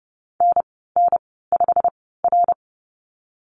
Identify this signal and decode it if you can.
Morse code, heard as a keyed tone, about 30 wpm, spaced as for 17 wpm, 700 Hz. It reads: NN5R
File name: Lesson 15g - Number 5 - Call Signs 30wpm